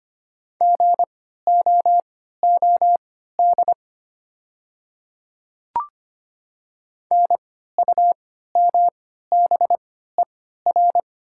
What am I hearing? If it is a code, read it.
Morse code, heard as a keyed tone, about 25 wpm, spaced as for 14 wpm, 700 Hz. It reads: GOOD NUMBER
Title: 3x Character Spacing - Top 200 Words - 25wpm